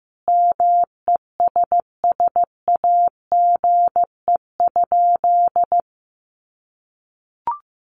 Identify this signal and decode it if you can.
Morse code, heard as a keyed tone, about 15 wpm, 700 Hz. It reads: MESSAGE?
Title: Sentences from Top 1100 Words 15wpm